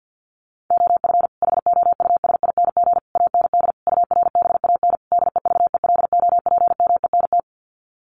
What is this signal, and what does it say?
Morse code, heard as a keyed tone, about 50 wpm, 700 Hz. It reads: OF HOUSING AND URBAN DEVELOPMENT
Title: Sets of 5 Words Encore2 50wpm